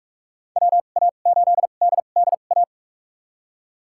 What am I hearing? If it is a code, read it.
Morse code, heard as a keyed tone, about 46 wpm, spaced as for 33 wpm, 700 Hz. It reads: WA8DDA
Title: USA Call Signs - 2x Character Spacing - 45wpm